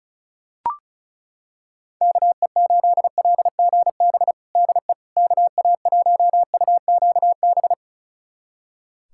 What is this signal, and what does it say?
Morse code, heard as a keyed tone, about 36 wpm, 700 Hz. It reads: KE8LGB DE KA1UQB
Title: USA Callsign DE Callsign - 6 Characters - 35wpm